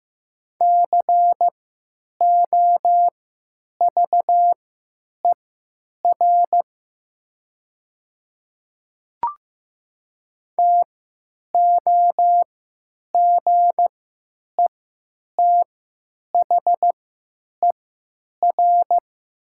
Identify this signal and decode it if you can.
Morse code, heard as a keyed tone, about 15 wpm, spaced as for 9 wpm, 700 Hz. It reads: COVER TOGETHER
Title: 3x Character Spacing - Top 300 Words - 15wpm